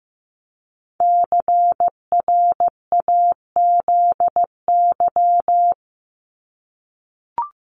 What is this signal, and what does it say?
Morse code, heard as a keyed tone, about 15 wpm, 700 Hz. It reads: CRAZY
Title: Top 1501-1750 Words 15wpm